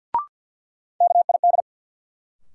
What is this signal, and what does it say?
Morse code, heard as a keyed tone, about 52 wpm, 700 Hz. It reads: KID
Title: Sets of 5 Words - New Word Drill - 50wpm